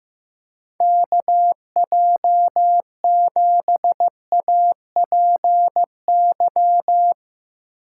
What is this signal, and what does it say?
Morse code, heard as a keyed tone, about 15 wpm, 700 Hz. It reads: KJ7APY
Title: USA Call Signs - 6 Characters - 4x Character Spacing - 15wpm